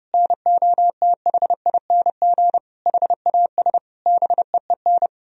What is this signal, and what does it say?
Morse code, heard as a keyed tone, about 30 wpm, 700 Hz. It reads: NOTHING HAS BEEN
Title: Sentences from Top 1000 Words - Review Part 2 - 30wpm